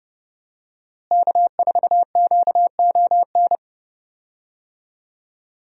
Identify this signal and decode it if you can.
Morse code, heard as a keyed tone, about 32 wpm, 700 Hz. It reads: K4QON
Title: USA Call Signs - 5 Characters - Repeated 3x - 30wpm